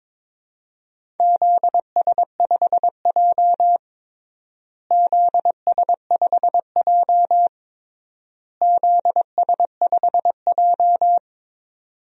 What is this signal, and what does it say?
Morse code, heard as a keyed tone, about 22 wpm, 700 Hz. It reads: ZS5J ZS5J ZS5J
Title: DXCC-100 Call Signs - 4 Characters - Repeated 3x - 22wpm